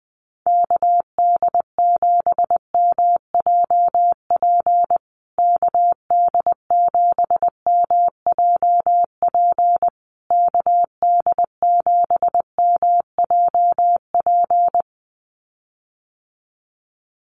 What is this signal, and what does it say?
Morse code, heard as a keyed tone, about 20 wpm, 700 Hz. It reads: KD7MJP KD7MJP KD7MJP
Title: USA Call Signs Repeated 3x 20wpm